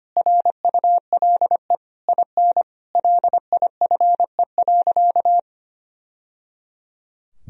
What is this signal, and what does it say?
Morse code, heard as a keyed tone, about 25 wpm, 700 Hz. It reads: RULE IN LIFE.
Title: Sentences from Top 1000 Words - Review Part 2 - 25wpm